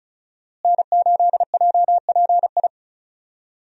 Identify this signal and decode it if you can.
Morse code, heard as a keyed tone, about 36 wpm, 700 Hz. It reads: N8JPI
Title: USA Call Signs - 5 Characters - Repeated 2x - 35wpm